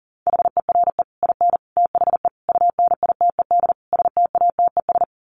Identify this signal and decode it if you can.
Morse code, heard as a keyed tone, about 40 wpm, 700 Hz. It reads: HERE IN THE UNITED STATES
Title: Sets of 5 Words 40wpm